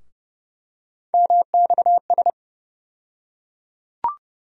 Morse code, heard as a keyed tone, about 30 wpm, 700 Hz. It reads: MXS
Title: USA Call Sign Postfixes - 3 Letters 30wpm